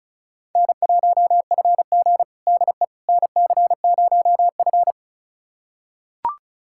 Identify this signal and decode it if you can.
Morse code, heard as a keyed tone, about 34 wpm, 700 Hz. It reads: N1FG DE NC0F
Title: USA Callsign DE Callsign - 4 Characters - 35wpm